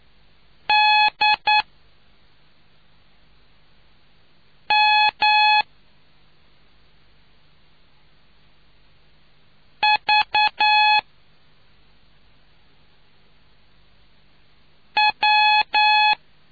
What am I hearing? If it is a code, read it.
Morse code, heard as a keyed tone, about 9 wpm, spaced as for 2 wpm, 850 Hz. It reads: DMVW